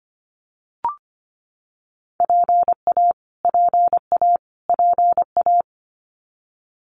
Morse code, heard as a keyed tone, about 25 wpm, 700 Hz. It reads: PA PA PA
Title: US States Repeated 3x 25wpm